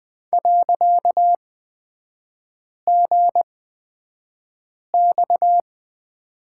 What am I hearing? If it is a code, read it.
Morse code, heard as a keyed tone, about 20 wpm, spaced as for 5 wpm, 690 Hz. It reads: .GX
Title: 255 Lesson 41e - Period - Three Characters 20wpm